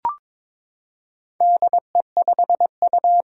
Morse code, heard as a keyed tone, about 22 wpm, 700 Hz. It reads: DE5U
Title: Four Challenging Letter-Number Combinations 22wpm